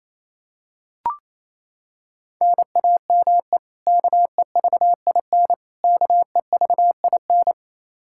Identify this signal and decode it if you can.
Morse code, heard as a keyed tone, about 28 wpm, 700 Hz. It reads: NAME KEVIN KEVIN
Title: Practice QSO - Exchange 1 - Beginning - 28wpm